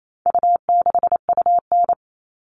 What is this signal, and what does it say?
Morse code, heard as a keyed tone, about 28 wpm, 700 Hz. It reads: U6UN
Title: All Cut Numbers - 4-Digit 28wpm